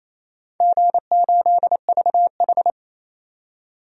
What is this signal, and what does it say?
Morse code, heard as a keyed tone, about 28 wpm, 700 Hz. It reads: G8VH
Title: DXCC-100 Call Signs - 4 Characters - 28wpm